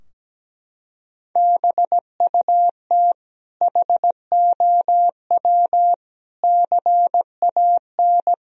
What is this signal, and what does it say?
Morse code, heard as a keyed tone, about 17 wpm, 700 Hz. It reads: BUT HOW CAN
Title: Sentences from Top 1000 Words - Review Part 4 - 17wpm